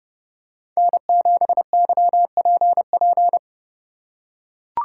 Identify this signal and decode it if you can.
Morse code, heard as a keyed tone, about 29 wpm, 700 Hz. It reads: N7YPP